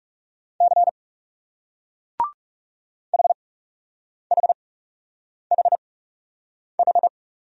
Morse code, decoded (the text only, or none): C HHHH